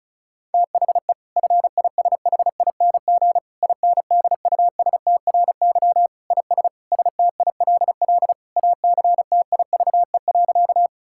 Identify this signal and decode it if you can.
Morse code, heard as a keyed tone, about 36 wpm, 700 Hz. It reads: THE FISHING INDUSTRY IS STILL ACTIVE.
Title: Sentences from Top 1000 Words - Review Part 6 - 35wpm